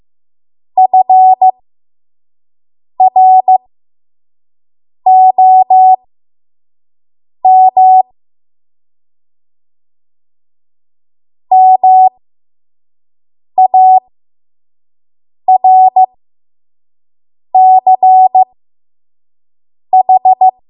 Morse code, heard as a keyed tone, about 15 wpm, spaced as for 5 wpm, 750 Hz. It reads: FROM MARCH
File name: Archive / 5 WPM / 05WPM